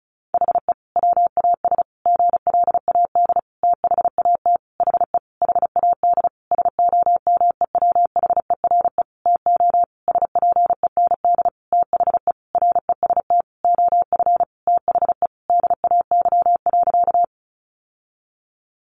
Morse code, heard as a keyed tone, about 34 wpm, 700 Hz. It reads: HE WAS GLAD THAT HE HAD SOMEWHERE TO SPEND THE REST OF THE DAY.
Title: Sentences from Top 1400 Words 35wpm